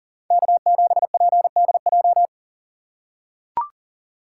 Morse code, heard as a keyed tone, about 40 wpm, 700 Hz. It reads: K7PDJ
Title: USA Call Signs - 2x Character Spacing - 40wpm